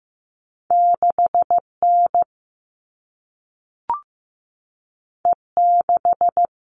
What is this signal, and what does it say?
Morse code, heard as a keyed tone, about 15 wpm, 700 Hz. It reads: 6N E6
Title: All Cut Numbers - 2-Digit 15wpm